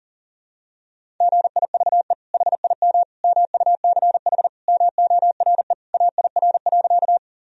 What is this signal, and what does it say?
Morse code, heard as a keyed tone, about 37 wpm, 700 Hz. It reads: GIVE HIM MUCH MORE AIR.
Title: Sentences from Top 1000 Words - Review Part 1 - 40wpm